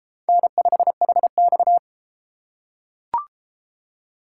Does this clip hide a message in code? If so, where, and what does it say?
Yes: Morse code from 0.3 to 1.8 s, heard as a keyed tone, about 30 wpm, 700 Hz. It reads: N5HX